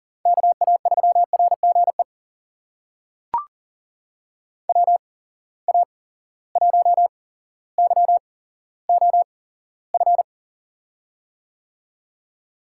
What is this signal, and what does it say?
Morse code, heard as a keyed tone, about 40 wpm, spaced as for 17 wpm, 700 Hz. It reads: KA3RGE WA1YOF
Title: USA Call Signs - 6 Characters - 8x Character Spacing - 40wpm